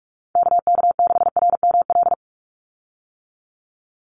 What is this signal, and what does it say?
Morse code, heard as a keyed tone, about 46 wpm, 700 Hz. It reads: KK6RML
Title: USA Call Signs 45wpm